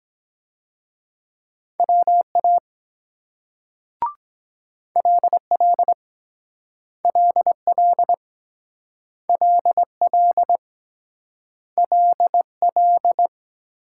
Morse code, heard as a keyed tone, about 23 wpm, 700 Hz. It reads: WA LL LL LL LL